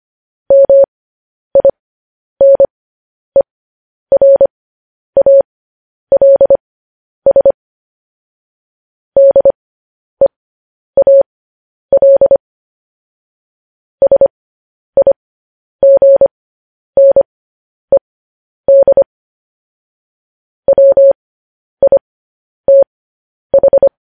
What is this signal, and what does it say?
Morse code, heard as a keyed tone, about 25 wpm, spaced as for 10 wpm, 550 Hz. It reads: MINERALS DEAL SIGNED WITH
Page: news headlines in morse code at 10 wpm